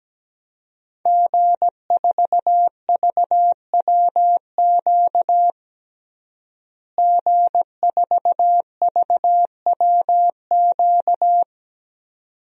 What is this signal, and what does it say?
Morse code, heard as a keyed tone, about 17 wpm, 700 Hz. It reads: G4VWQ G4VWQ
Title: DXCC-100 Call Signs - 5 Characters - Repeated 2x - 17wpm